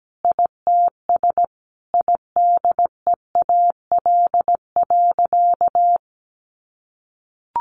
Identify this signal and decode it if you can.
Morse code, heard as a keyed tone, about 17 wpm, 700 Hz. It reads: ITS IDEAL.
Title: Sentences from Top 1400 Words 17wpm